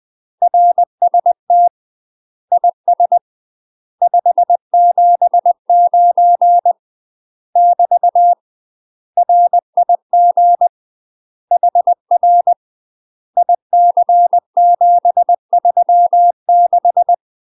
Morse code, heard as a keyed tone, about 20 wpm, 700 Hz. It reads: RST IS 579 <BT> RIG HR IC736
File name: UK MASTER QSOs TRAINING / 20WPM / UK MASTER QSOs 0053